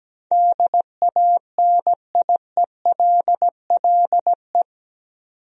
Morse code, heard as a keyed tone, about 17 wpm, 700 Hz. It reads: DANIELLE
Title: USA Common Names - 17wpm